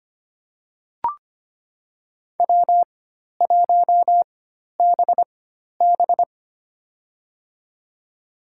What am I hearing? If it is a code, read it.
Morse code, heard as a keyed tone, about 25 wpm, spaced as for 12 wpm, 700 Hz. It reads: W1BB
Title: USA Call Signs - 4 Characters - 4x Character Spacing - 25wpm